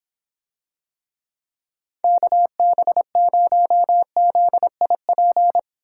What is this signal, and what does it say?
Morse code, heard as a keyed tone, about 26 wpm, 700 Hz. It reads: KB0ZIP